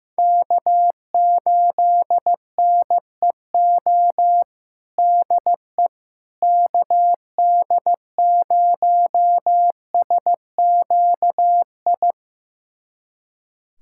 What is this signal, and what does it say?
Morse code, heard as a keyed tone, about 15 wpm, 700 Hz. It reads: K8NEO DE KD0SQI
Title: USA Callsign DE Callsign 15wpm